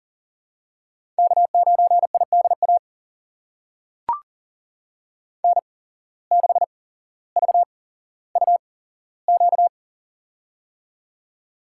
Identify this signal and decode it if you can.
Morse code, heard as a keyed tone, about 40 wpm, spaced as for 17 wpm, 700 Hz. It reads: K9IDA N6VUQ
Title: USA Call Signs - 5 Characters - 8x Character Spacing - 40wpm